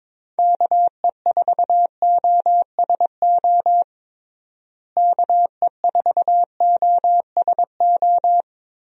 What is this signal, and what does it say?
Morse code, heard as a keyed tone, about 22 wpm, 700 Hz. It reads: KE4OSO KE4OSO
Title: USA Call Signs - 6 Characters - Repeated 3x - 22wpm